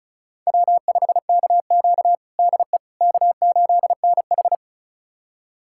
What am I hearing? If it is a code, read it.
Morse code, heard as a keyed tone, about 34 wpm, 700 Hz. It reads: W5KQ DE K8NH
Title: USA Callsign DE Callsign - 4 Characters - 35wpm